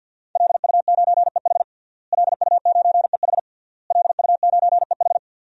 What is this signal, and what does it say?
Morse code, heard as a keyed tone, about 52 wpm, 700 Hz. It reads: LU9EH LU9EH LU9EH
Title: DXCC-100 Call Signs - 5 Characters - Repeated 3x - 50wpm